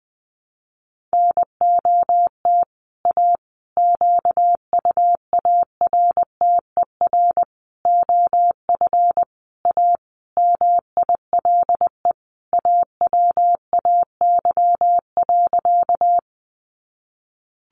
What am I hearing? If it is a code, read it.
Morse code, heard as a keyed tone, about 20 wpm, 700 Hz. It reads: NOT A QUARTER OF A MILE AWAY.
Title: Sentences from Top 1300 Words 20wpm